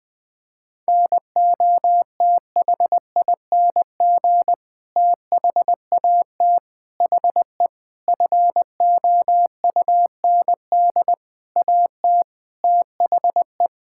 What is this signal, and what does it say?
Morse code, heard as a keyed tone, about 20 wpm, 700 Hz. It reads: NOTHING THAT HE FOUND AT THE
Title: Sentences from Top 1000 Words - Review Part 6 - 20wpm